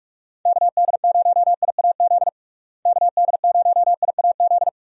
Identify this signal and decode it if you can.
Morse code, heard as a keyed tone, about 45 wpm, 700 Hz. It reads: KD0IAZ KD0IAZ
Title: USA Call Signs - 6 Characters - Repeated 2x - 45wpm